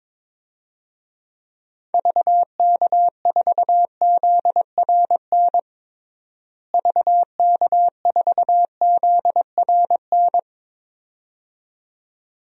Morse code, heard as a keyed tone, about 22 wpm, 700 Hz. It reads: VK4ZRN VK4ZRN
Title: DXCC-100 Call Signs - 6 Characters - Repeated 2x - 22wpm